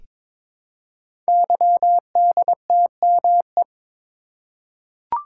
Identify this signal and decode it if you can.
Morse code, heard as a keyed tone, about 23 wpm, 700 Hz. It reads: YDTME
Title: Code Groups - 5 Letters - 22wpm